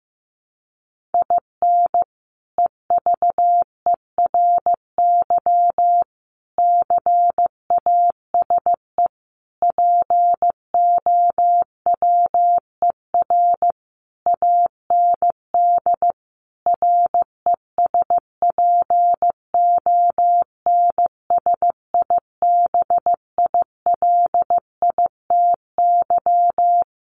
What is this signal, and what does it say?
Morse code, heard as a keyed tone, about 15 wpm, 700 Hz. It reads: IN EVERY CASE POWER AND RESPONSIBILITY
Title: Sentences from Top 1200 Words 15wpm